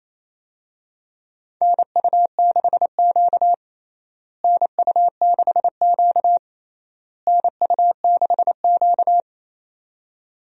Morse code, heard as a keyed tone, about 28 wpm, 700 Hz. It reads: NU6Q NU6Q NU6Q
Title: USA Call Signs - 4 Characters - Repeated 3x - 28wpm